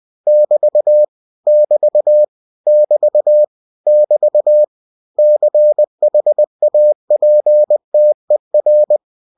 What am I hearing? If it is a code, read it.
Morse code, heard as a keyed tone, about 20 wpm, 600 Hz. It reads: <BT> <BT> <BT> <BT> CHAPTER
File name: cw-books / winnie-the-pooh / winnie-20wpm-page-0020